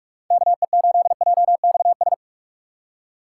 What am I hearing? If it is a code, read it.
Morse code, heard as a keyed tone, about 46 wpm, 700 Hz. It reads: KE8JXS
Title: USA Call Signs - 6 Characters - 6x Character Spacing - 45wpm